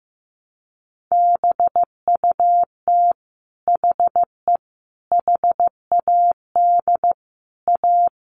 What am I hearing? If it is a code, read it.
Morse code, heard as a keyed tone, about 15 wpm, 700 Hz. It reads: BUT HE HAD A
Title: Sentences from Top 1200 Words 15wpm